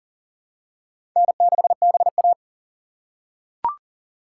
Morse code, heard as a keyed tone, about 40 wpm, 700 Hz. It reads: N6BA
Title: USA Call Signs - 4 Characters - 3x Character Spacing - 40wpm